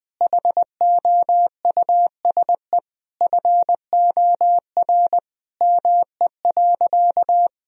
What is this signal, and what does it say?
Morse code, heard as a keyed tone, about 20 wpm, 700 Hz. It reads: HOUSE FOR ME.
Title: Sentences from Top 1000 Words - Review Part 3 - 20wpm